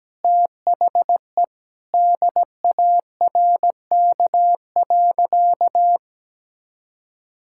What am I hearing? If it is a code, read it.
Morse code, heard as a keyed tone, about 17 wpm, 700 Hz. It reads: THE DARK.
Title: Sentences from Top 1600 Words 17wpm